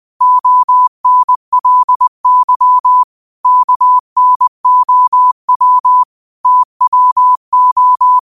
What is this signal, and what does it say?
Morse code, heard as a keyed tone, about 20 wpm, 1000 Hz. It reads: ONLY KNOW TWO
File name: 20wpm